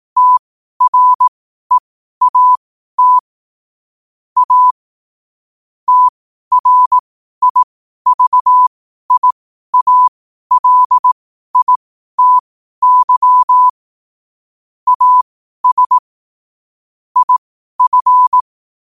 Morse code, heard as a keyed tone, about 18 wpm, spaced as for 12 wpm, 1000 Hz. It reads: TREAT A TRIVIALITY AS IF
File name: qotdfiles / 12wpm